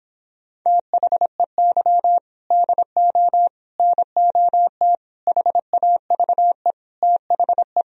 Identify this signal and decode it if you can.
Morse code, heard as a keyed tone, about 26 wpm, 700 Hz. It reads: THEY DO NOT HAVE THE